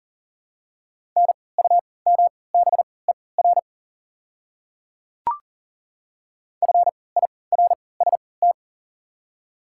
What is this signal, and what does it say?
Morse code, heard as a keyed tone, about 40 wpm, spaced as for 23 wpm, 700 Hz. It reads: NUMBER FIRST
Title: 3x Character Spacing - Top 200 Words - 40wpm